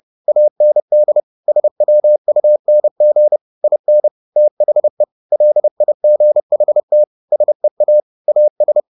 Morse code, heard as a keyed tone, about 30 wpm, 600 Hz. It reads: AND SWUNG IN THE LIGHT SEA AS
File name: the old man and the sea 2 30wpm